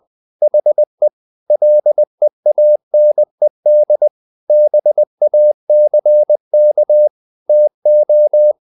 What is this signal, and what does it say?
Morse code, heard as a keyed tone, about 20 wpm, 600 Hz. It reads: HE LEANED BACK TO
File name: the old man and the sea / the old man and the sea part 1 20wpm